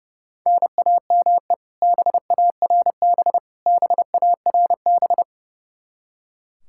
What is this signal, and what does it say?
Morse code, heard as a keyed tone, about 29 wpm, 700 Hz. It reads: NAME BARB BARB
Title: Practice QSO - Exchange 1 - Beginning - 30wpm